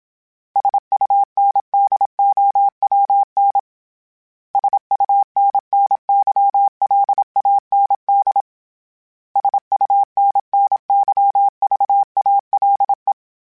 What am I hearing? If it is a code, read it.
Morse code, heard as a keyed tone, about 27 wpm, 800 Hz. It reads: SUNDOWN SUNNYLAND SUNNYVALE
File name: fromSMs / FromSMs mp3s / Cities@25